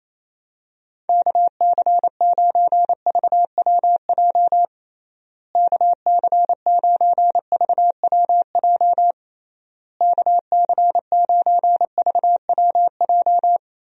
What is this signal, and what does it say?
Morse code, heard as a keyed tone, about 27 wpm, 700 Hz. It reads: KC9VWJ KC9VWJ KC9VWJ
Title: USA Call Signs - 6 Characters - Repeated 3x - 28wpm